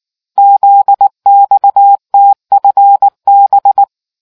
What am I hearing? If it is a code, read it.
Morse code, heard as a keyed tone, about 19 wpm, 800 Hz. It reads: ZXTFB